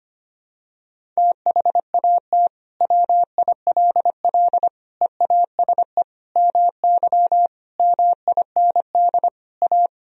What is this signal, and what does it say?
Morse code, heard as a keyed tone, about 25 wpm, 700 Hz. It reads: THAT WILL EASE MY MIND A
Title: Sentences from Top 1000 Words - Review Part 1 - 25wpm